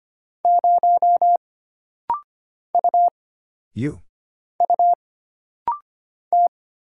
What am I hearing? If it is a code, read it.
Morse code, heard as a keyed tone, about 25 wpm, 700 Hz. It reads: 0 U U T
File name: Single Character (Letter:Number:Punctuation:Prosign) - Rapid-Fire - 25wpm